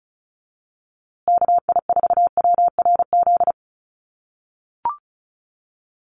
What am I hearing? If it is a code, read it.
Morse code, heard as a keyed tone, about 34 wpm, 700 Hz. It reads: KI4WRZ
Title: USA Call Signs Repeated 3x 35wpm